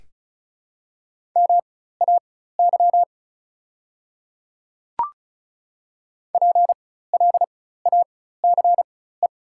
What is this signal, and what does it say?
Morse code, heard as a keyed tone, about 34 wpm, spaced as for 16 wpm, 700 Hz. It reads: MAY PLACE
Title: 4x Character Spacing - Top 300 Words - 35wpm